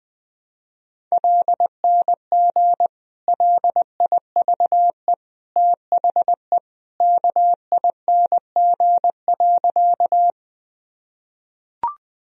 Morse code, heard as a keyed tone, about 20 wpm, 700 Hz. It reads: LNG LIVE THE KING.
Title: Sentences from Top 400 Words and Abbreviations 20wpm